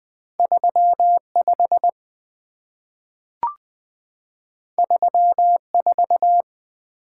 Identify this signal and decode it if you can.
Morse code, heard as a keyed tone, about 20 wpm, 700 Hz. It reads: 35 34
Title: Two Letter-Number Combinations - 20wpm